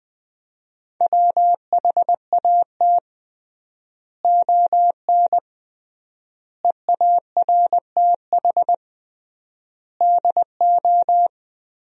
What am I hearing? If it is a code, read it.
Morse code, heard as a keyed tone, about 20 wpm, 700 Hz. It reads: WHAT ON EARTH DO
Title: 3x Word-Spacing - Sentences from Top 200 Words - 20wpm